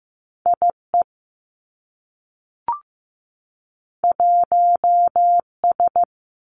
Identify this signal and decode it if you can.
Morse code, heard as a keyed tone, about 15 wpm, 700 Hz. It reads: I E 1 S
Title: Two Challenging Letter-Number Combinations 15wpm